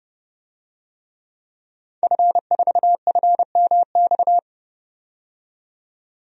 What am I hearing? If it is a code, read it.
Morse code, heard as a keyed tone, about 32 wpm, 700 Hz. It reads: F4FMX
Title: DXCC-100 Call Signs - 5 Characters - 30wpm